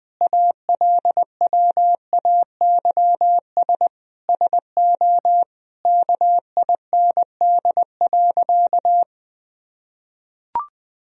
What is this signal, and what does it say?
Morse code, heard as a keyed tone, about 21 wpm, 700 Hz. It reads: ALWAYS SO KIND.
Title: Sentences from Top 1000 Words - Review Part 3 - 20wpm